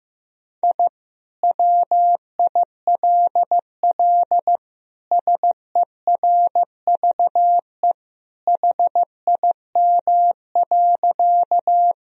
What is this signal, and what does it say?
Morse code, heard as a keyed tone, about 15 wpm, 700 Hz. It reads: I WILL SERVE HIM.
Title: Sentences from Top 1000 Words - Review Part 2 - 15wpm